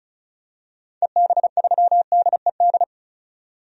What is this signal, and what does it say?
Morse code, heard as a keyed tone, about 34 wpm, 700 Hz. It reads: EB3DED
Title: DXCC-100 Call Signs - 6 Characters - 35wpm